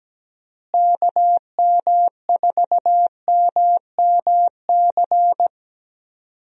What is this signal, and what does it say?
Morse code, heard as a keyed tone, about 17 wpm, 700 Hz. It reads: KM4MMC
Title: USA Call Signs - 6 Characters - Repeated 2x - 17wpm